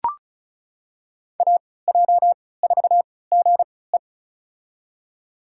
Morse code, heard as a keyed tone, about 36 wpm, spaced as for 20 wpm, 700 Hz. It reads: AJ4GE
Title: USA Call Signs - 5 Characters - 3x Character Spacing - 35wpm